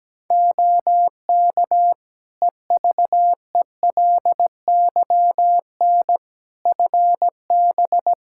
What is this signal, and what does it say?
Morse code, heard as a keyed tone, about 17 wpm, 700 Hz. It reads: OK EVELYN FB
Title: Practice QSO - Exchange 2 - Info - 17wpm